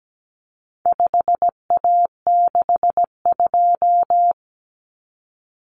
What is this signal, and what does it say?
Morse code, heard as a keyed tone, about 17 wpm, 700 Hz. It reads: 5A62
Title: Common Cut Numbers - 4-Digit 17wpm